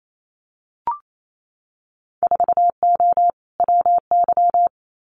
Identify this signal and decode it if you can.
Morse code, heard as a keyed tone, about 28 wpm, 700 Hz. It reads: 4O WY
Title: Winter Field Day Exchanges Encore 28wpm